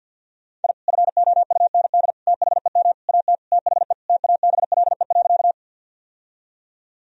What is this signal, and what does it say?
Morse code, heard as a keyed tone, about 52 wpm, 700 Hz. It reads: I FOUND THEM AT THE TABLE.
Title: Sentences from Top 1000 Words - Review Part 2 - 50wpm